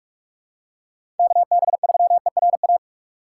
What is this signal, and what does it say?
Morse code, heard as a keyed tone, about 45 wpm, 700 Hz. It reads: KB3ERA
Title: USA Call Signs - 6 Characters - 8x Character Spacing - 45wpm